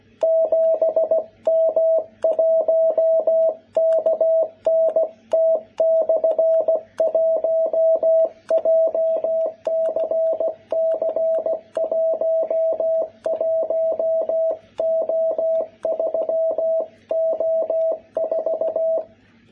Morse code, heard as a keyed tone, about 16 wpm, 660 Hz. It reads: 7M1KNT/1 JCC11O3O4
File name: jh8cbh / 2021cw